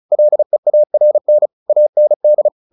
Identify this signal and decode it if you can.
Morse code, heard as a keyed tone, about 34 wpm, 600 Hz. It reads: LEARN AND
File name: the old man and the sea 2 35wpm